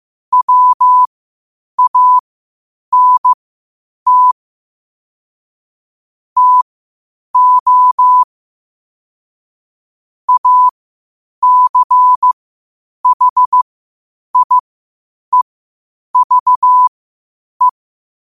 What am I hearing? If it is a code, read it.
Morse code, heard as a keyed tone, about 15 wpm, spaced as for 8 wpm, 1000 Hz. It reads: WANT TO ACHIEVE
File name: qotdfiles / 8wpm